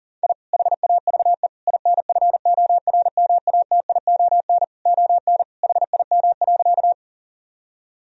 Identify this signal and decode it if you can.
Morse code, heard as a keyed tone, about 37 wpm, 700 Hz. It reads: I HAVE INFORMATION ON HIM.
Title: Sentences from Top 1000 Words - Review Part 3 - 40wpm